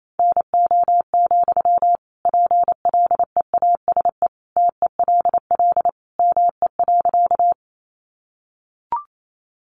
Morse code, heard as a keyed tone, about 28 wpm, 700 Hz. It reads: NO, PLEASE TELL ME.
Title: Sentences from Top 1500 Words 28wpm